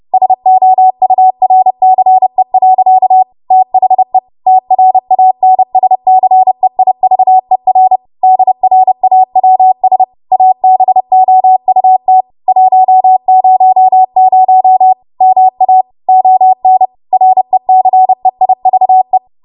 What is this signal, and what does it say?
Morse code, heard as a keyed tone, about 30 wpm, 750 Hz. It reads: SOURCE. THE TRANSCEIVER DRAWS ABOUT 100 MA ON RECEIVE